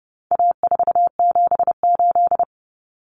Morse code, heard as a keyed tone, about 30 wpm, 700 Hz. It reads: A478
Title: Common Cut Numbers - 4-Digit 30wpm